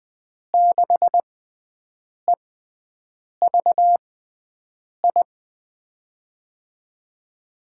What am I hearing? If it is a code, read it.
Morse code, heard as a keyed tone, about 20 wpm, spaced as for 7 wpm, 700 Hz. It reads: 6EVI
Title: USA Call Signs - 6 Characters - 6x Character Spacing - 20wpm